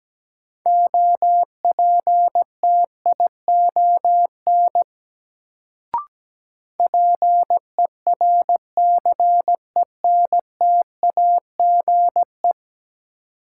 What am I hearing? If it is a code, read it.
Morse code, heard as a keyed tone, about 17 wpm, 700 Hz. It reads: OPTION PERCENTAGE
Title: Sentences from Top 1300 Words - New Word Drill - 17wpm